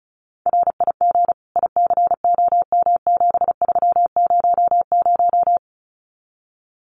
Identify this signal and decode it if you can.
Morse code, heard as a keyed tone, about 34 wpm, 700 Hz. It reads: RIG ICOM7300
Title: Common QSO Phrases 35wpm